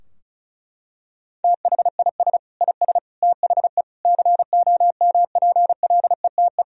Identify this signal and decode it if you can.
Morse code, heard as a keyed tone, about 36 wpm, 700 Hz. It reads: THIS IS THE COMPLETE
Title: Sentences from Top 1000 Words - Review Part 2 - 35wpm